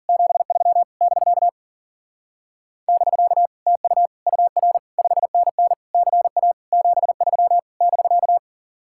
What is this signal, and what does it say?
Morse code, heard as a keyed tone, about 49 wpm, 700 Hz. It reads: 73 <BK> <BK> TU UR 5NN CA 73 <BK>
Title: Speed-Racing Common QSO Phrases - 40wpm